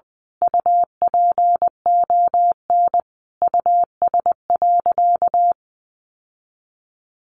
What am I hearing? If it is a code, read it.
Morse code, heard as a keyed tone, about 20 wpm, 700 Hz. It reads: UPON US.
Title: Sentences from Top 1500 Words 20wpm